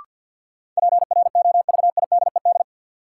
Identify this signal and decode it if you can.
Morse code, heard as a keyed tone, about 53 wpm, 700 Hz. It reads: PROVIDED
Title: Sentences from Top 1500 Words - New Word Drill - 50wpm